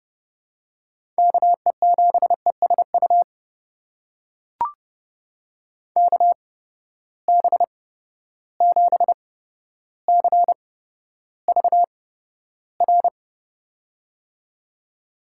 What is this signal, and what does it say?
Morse code, heard as a keyed tone, about 30 wpm, spaced as for 13 wpm, 700 Hz. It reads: KE7ESU KB7CVR
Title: USA Call Signs - 6 Characters - 8x Character Spacing - 30wpm